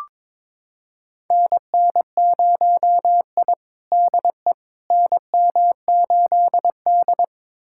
Morse code, heard as a keyed tone, about 21 wpm, 700 Hz. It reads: NN0I DE NM8D
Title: USA Callsign DE Callsign - 4 Characters - 22wpm